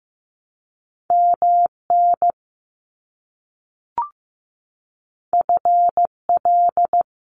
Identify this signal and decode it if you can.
Morse code, heard as a keyed tone, about 15 wpm, 700 Hz. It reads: MN FL
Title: US States 15wpm